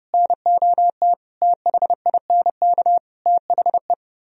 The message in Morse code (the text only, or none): NOT THINK THE